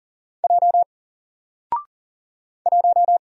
Morse code, heard as a keyed tone, about 40 wpm, 700 Hz. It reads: J 1
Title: Single Letter-Number Alt 40wpm